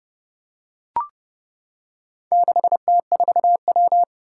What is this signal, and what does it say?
Morse code, heard as a keyed tone, about 31 wpm, 700 Hz. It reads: 6T4W
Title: Four Challenging Letter-Number Combinations 30wpm